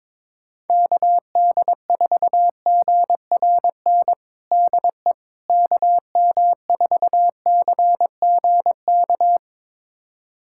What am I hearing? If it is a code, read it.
Morse code, heard as a keyed tone, about 22 wpm, 700 Hz. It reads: KD4GRN DE KM4CGK
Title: USA Callsign DE Callsign - 6 Characters - 22wpm